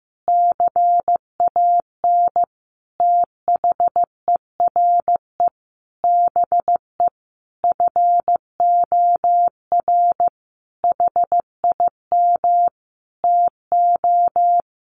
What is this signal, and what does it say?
Morse code, heard as a keyed tone, about 15 wpm, 700 Hz. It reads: CAN THERE BE FOR HIM TO
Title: Sentences from Top 1300 Words 15wpm